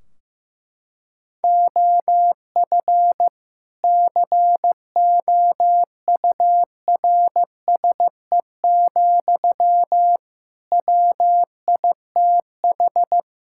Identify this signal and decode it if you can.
Morse code, heard as a keyed tone, about 15 wpm, 700 Hz. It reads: OF COURSE, WITH
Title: Sentences from Top 1600 Words 15wpm